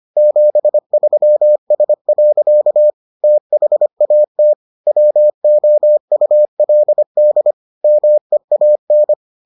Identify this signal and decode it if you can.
Morse code, heard as a keyed tone, about 25 wpm, 600 Hz. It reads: 73S. THAT WOULD MEAN